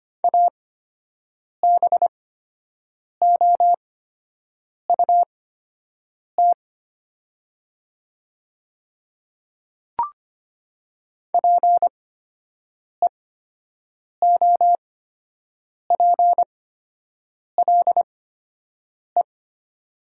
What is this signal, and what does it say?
Morse code, heard as a keyed tone, about 25 wpm, spaced as for 7 wpm, 700 Hz. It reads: ABOUT PEOPLE
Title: 8x Character Spacing - Top 100 Words - 25wpm